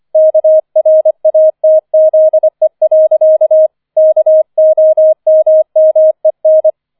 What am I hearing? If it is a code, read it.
Morse code, heard as a keyed tone, about 25 wpm, 630 Hz. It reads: KRATZE. KOMMEN
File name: 005 Fünfter Streich 25wpm